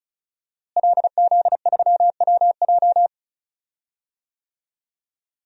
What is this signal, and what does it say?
Morse code, heard as a keyed tone, about 36 wpm, 700 Hz. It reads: LZ3WJ
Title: DXCC-100 Call Signs - 5 Characters - Repeated 2x - 35wpm